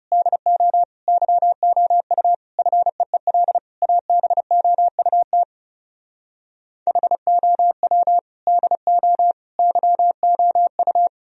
Morse code, heard as a keyed tone, about 40 wpm, 700 Hz. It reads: DO YOU FEEL ABOUT HOW DO YOU